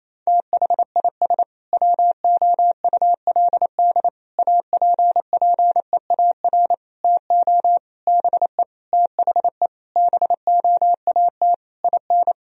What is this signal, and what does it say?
Morse code, heard as a keyed tone, about 28 wpm, 700 Hz. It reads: THIS WOULD APPEAR TO BE THE BOAT IN
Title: Sentences from Top 1000 Words - Review Part 3 - 28wpm